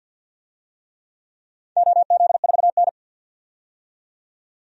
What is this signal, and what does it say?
Morse code, heard as a keyed tone, about 51 wpm, 700 Hz. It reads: OZ4N
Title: DXCC-100 Call Signs - 4 Characters - 50wpm